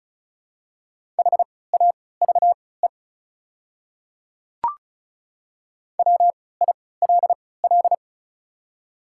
Morse code, heard as a keyed tone, about 36 wpm, spaced as for 20 wpm, 700 Hz. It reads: HAVE WILL
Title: 3x Character Spacing - Top 100 Words - 35wpm